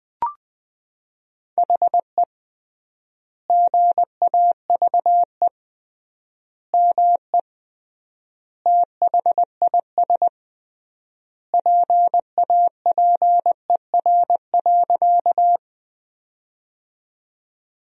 Morse code, heard as a keyed tone, about 20 wpm, 700 Hz. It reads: HE GAVE ME THIS PAPER.
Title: 3x Word-Spacing - Sentences from Top 500 Words - 20wpm